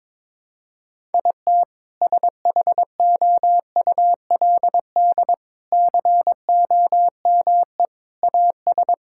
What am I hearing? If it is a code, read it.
Morse code, heard as a keyed tone, about 22 wpm, 700 Hz. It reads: IT SHOULD COME AS
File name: practice / 22wpm / Sets of 6 Words - 22wpm